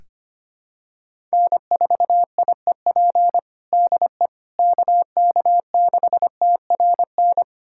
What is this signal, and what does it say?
Morse code, heard as a keyed tone, about 25 wpm, 700 Hz. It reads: N4IEP DE KK6TRN
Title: USA Callsign DE Callsign 25wpm